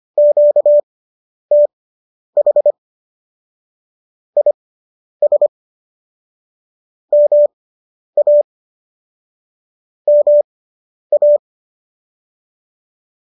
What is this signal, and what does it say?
Morse code, heard as a keyed tone, about 25 wpm, spaced as for 10 wpm, 590 Hz. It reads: QTH IS MA MA